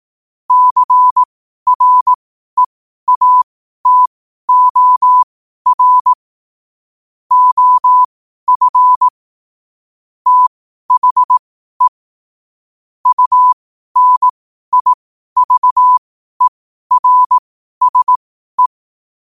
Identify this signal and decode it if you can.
Morse code, heard as a keyed tone, about 18 wpm, spaced as for 12 wpm, 1000 Hz. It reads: CREATOR OF THE UNIVERSE
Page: 12 wpm morse code quotes